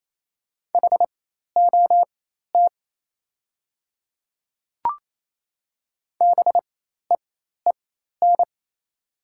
Morse code, heard as a keyed tone, about 28 wpm, spaced as for 13 wpm, 700 Hz. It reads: HOT BEEN
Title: 4x Character Spacing - Top 100 Words - 28wpm